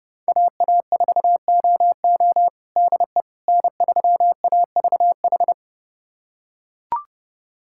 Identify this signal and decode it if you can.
Morse code, heard as a keyed tone, about 32 wpm, 700 Hz. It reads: AA4OO DE N3AVH
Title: USA Callsign DE Callsign - 5 Characters - 30wpm